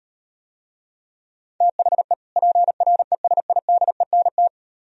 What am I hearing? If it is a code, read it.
Morse code, heard as a keyed tone, about 38 wpm, 700 Hz. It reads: THE PRESIDENT